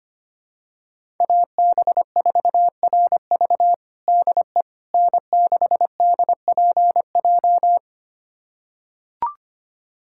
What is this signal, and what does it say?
Morse code, heard as a keyed tone, about 25 wpm, 700 Hz. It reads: AB4RV DE N6DPJ
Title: USA Callsign DE Callsign - 5 Characters - 25wpm